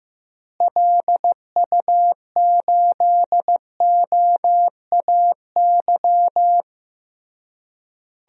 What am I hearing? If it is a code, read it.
Morse code, heard as a keyed tone, about 15 wpm, 700 Hz. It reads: LU8OAY
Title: DXCC-100 Call Signs - 6 Characters - Repeated 3x - 15wpm